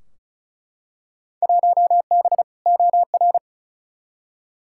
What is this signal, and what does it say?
Morse code, heard as a keyed tone, about 36 wpm, 700 Hz. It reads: 1B OR
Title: Field Day Exchanges Encore - 35wpm